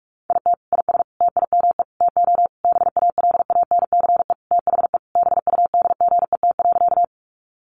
Morse code, heard as a keyed tone, about 44 wpm, 700 Hz. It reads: IT IS TIME TO BALANCE THE BUDGET.
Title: Sentences from Top 1200 Words 45wpm